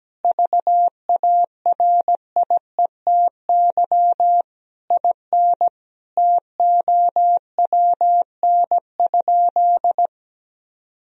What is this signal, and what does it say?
Morse code, heard as a keyed tone, about 17 wpm, 700 Hz. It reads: VARIETY IN TOWN?
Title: Sentences from Top 1000 Words - Review Part 4 - 17wpm